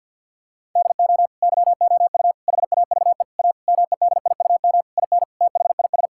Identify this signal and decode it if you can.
Morse code, heard as a keyed tone, about 50 wpm, 700 Hz. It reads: DO YOU HAVE A MEDIUM IN THIS
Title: Sentences from Top 1000 Words - Review Part 6 - 50wpm